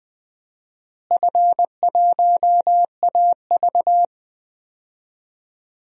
Morse code, heard as a keyed tone, about 20 wpm, 700 Hz. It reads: F1AV
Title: DXCC-100 Call Signs - 4 Characters - Repeated 3x - 20wpm